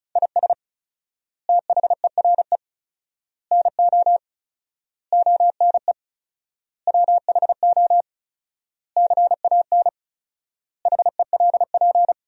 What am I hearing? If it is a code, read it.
Morse code, heard as a keyed tone, about 34 wpm, 700 Hz. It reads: IS THERE NO ONE WHO CAN HELP
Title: 4x Word-Spacing - Sentences from Top 200 Words - 35wpm